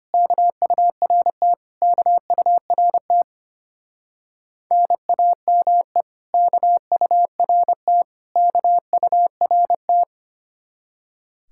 Morse code, heard as a keyed tone, about 31 wpm, 700 Hz. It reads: KURT KURT NAME KURT KURT
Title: Speed-Racing Common QSO Phrases - 25wpm